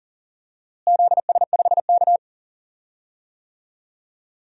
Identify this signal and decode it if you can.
Morse code, heard as a keyed tone, about 40 wpm, 700 Hz. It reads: ZS5K
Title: DXCC-100 Call Signs - 4 Characters - Repeated 3x - 40wpm